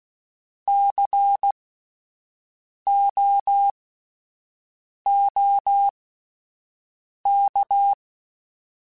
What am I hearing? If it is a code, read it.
Morse code, heard as a keyed tone, about 16 wpm, spaced as for 5 wpm, 800 Hz. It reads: COOK
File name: Alphabetical / Alphabetical mp3s / Cs @5